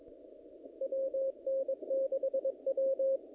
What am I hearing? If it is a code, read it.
Morse code, heard as a keyed tone, about 22 wpm, 540 Hz. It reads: WN6W
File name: contests / naqp-cw-k5tr-op-k5pi- -2000